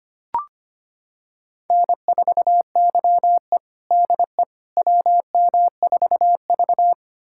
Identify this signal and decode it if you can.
Morse code, heard as a keyed tone, about 25 wpm, 700 Hz. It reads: N4YE DE WM4V
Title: USA Callsign DE Callsign - 4 Characters - 25wpm